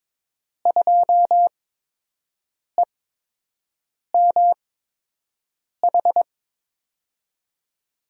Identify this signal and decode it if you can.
Morse code, heard as a keyed tone, about 22 wpm, spaced as for 6 wpm, 700 Hz. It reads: 2EMH
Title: USA Call Signs - 6 Characters - 8x Character Spacing - 22wpm